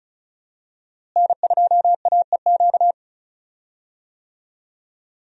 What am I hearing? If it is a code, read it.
Morse code, heard as a keyed tone, about 36 wpm, 700 Hz. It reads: N2AEQ
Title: USA Call Signs - 5 Characters - Repeated 3x - 35wpm